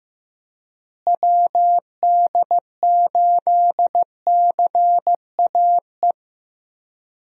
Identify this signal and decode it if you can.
Morse code, heard as a keyed tone, about 15 wpm, 700 Hz. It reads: WD8CAE
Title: USA Call Signs - 6 Characters - Repeated 3x - 15wpm